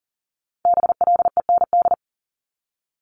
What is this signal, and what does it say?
Morse code, heard as a keyed tone, about 38 wpm, 700 Hz. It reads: BLEND
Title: Multi-letter Phonemes 40wpm